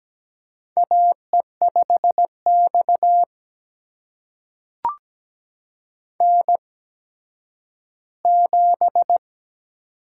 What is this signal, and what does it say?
Morse code, heard as a keyed tone, about 17 wpm, spaced as for 10 wpm, 700 Hz. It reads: AE5X N7
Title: USA Call Signs - 4 Characters - 8x Character Spacing - 17wpm